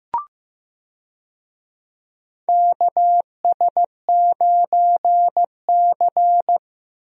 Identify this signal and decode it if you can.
Morse code, heard as a keyed tone, about 15 wpm, 700 Hz. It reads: KS9C
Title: USA Call Signs - 4 Characters - 15wpm